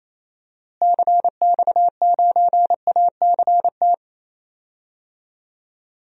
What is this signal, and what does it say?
Morse code, heard as a keyed tone, about 28 wpm, 700 Hz. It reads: CX9ACT
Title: DXCC-100 Call Signs - 6 Characters - Repeated 2x - 28wpm